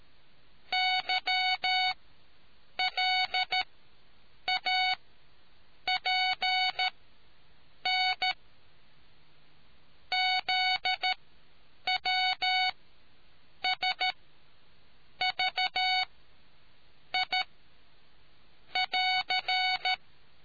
Morse code, heard as a keyed tone, about 13 wpm, spaced as for 7 wpm, 740 Hz. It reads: YLAPN ZWSVI <AR>